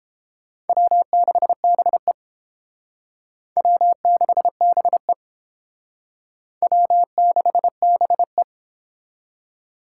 Morse code, heard as a keyed tone, about 31 wpm, 700 Hz. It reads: W6BE W6BE W6BE